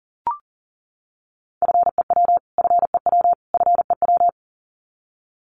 Morse code, heard as a keyed tone, about 40 wpm, 700 Hz. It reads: FEW FEW FEW
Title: Top 300 Words Repeated 40wpm